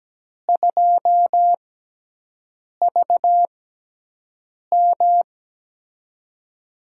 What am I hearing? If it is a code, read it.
Morse code, heard as a keyed tone, about 17 wpm, spaced as for 6 wpm, 700 Hz. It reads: 2VM
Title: USA Call Signs - 6x Character Spacing - 17wpm